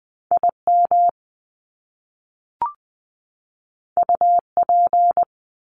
Morse code, heard as a keyed tone, about 20 wpm, 700 Hz. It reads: IM UP
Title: Top Two Letter Combinations in Words 20wpm